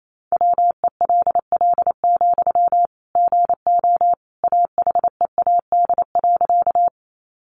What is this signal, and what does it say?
Morse code, heard as a keyed tone, about 28 wpm, 700 Hz. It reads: WELL, GO AHEAD.
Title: Sentences from Top 1100 Words 28wpm